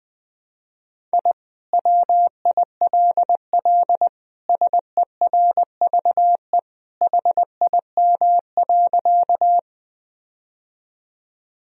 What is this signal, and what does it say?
Morse code, heard as a keyed tone, about 20 wpm, 700 Hz. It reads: I WILL SERVE HIM.
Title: Sentences from Top 1000 Words - Review Part 2 - 20wpm